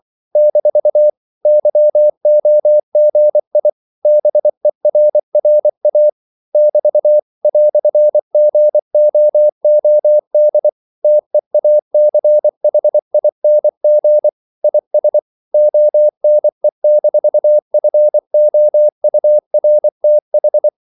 Morse code, heard as a keyed tone, about 24 wpm, 600 Hz. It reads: - YOGI BERRA <BT> "GOOD TEACHING IS ONE-FOURTH
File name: CW / Practice / EnglishQuotations0324